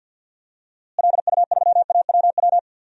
Morse code, heard as a keyed tone, about 50 wpm, 700 Hz. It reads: LU3AWW
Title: DXCC-100 Call Signs - 6 Characters - Repeated 2x - 50wpm